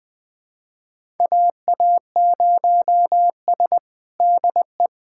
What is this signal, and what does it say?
Morse code, heard as a keyed tone, about 20 wpm, 700 Hz. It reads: AA0S DE
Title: USA Callsign DE Callsign - 4 Characters - 20wpm